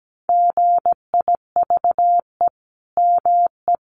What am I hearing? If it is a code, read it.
Morse code, heard as a keyed tone, about 17 wpm, 700 Hz. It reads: GIVE ME
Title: Sentences from Top 1500 Words 17wpm